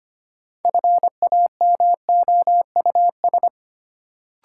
Morse code, heard as a keyed tone, about 25 wpm, 700 Hz. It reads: FAMOUS
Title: Sentences from Top 1500 Words - New Word Drill - 25wpm